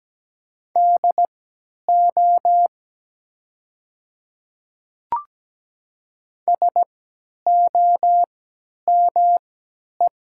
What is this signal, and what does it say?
Morse code, heard as a keyed tone, about 17 wpm, spaced as for 10 wpm, 700 Hz. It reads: DO SOME
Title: 3x Character Spacing - Top 200 Words - 17wpm